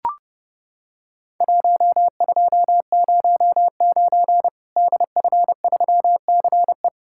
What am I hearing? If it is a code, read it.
Morse code, heard as a keyed tone, about 30 wpm, 700 Hz. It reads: 1209 DF3CE
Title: Worked All Europe DX Contest - QTC Exchange - 30wpm